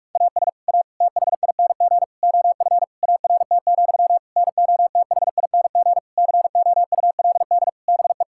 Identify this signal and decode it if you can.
Morse code, heard as a keyed tone, about 41 wpm, 700 Hz. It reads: AS A THING OF ART, NOTHING COULD BE
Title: Sentences from Top 1000 Words - Review Part 6 - 45wpm